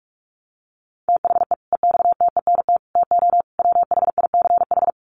Morse code, heard as a keyed tone, about 44 wpm, 700 Hz. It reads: THE EXTENT TO WHICH
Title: Sets of 4 Words 45wpm